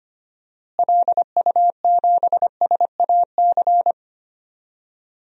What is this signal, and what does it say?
Morse code, heard as a keyed tone, about 25 wpm, 700 Hz. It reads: LU7SAC
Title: DXCC-100 Call Signs - 6 Characters - Repeated 3x - 25wpm